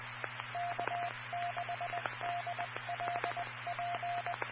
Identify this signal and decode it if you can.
Morse code, heard as a keyed tone, about 20 wpm, 700 Hz. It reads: K6DLP
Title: Claremont, CA Police, Amateur Radio